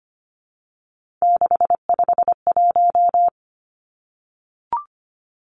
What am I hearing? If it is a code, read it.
Morse code, heard as a keyed tone, about 25 wpm, 690 Hz. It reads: 651
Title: Numbers - 3-Digit 25wpm